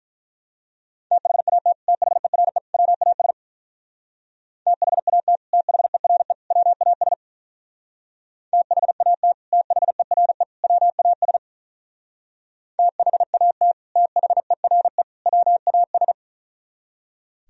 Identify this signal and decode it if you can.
Morse code, heard as a keyed tone, about 44 wpm, 700 Hz. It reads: THAT THERE WAS THAT THERE WAS THAT THERE WAS THAT THERE WAS